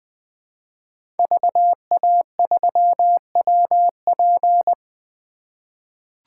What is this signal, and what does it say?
Morse code, heard as a keyed tone, about 20 wpm, 700 Hz. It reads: VA3WP
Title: 2023 POTA Activator Call Signs - 20wpm